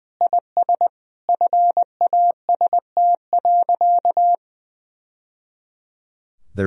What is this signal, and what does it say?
Morse code, heard as a keyed tone, about 21 wpm, 700 Hz. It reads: IS FAST.
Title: Sentences from Top 1000 Words - Review Part 4 - 20wpm